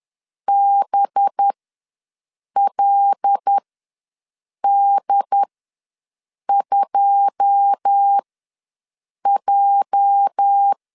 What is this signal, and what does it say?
Morse code, heard as a keyed tone, about 10 wpm, spaced as for 6 wpm, 800 Hz. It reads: BLD2J
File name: LEZIONI / 9 2 ws 10wpm 4130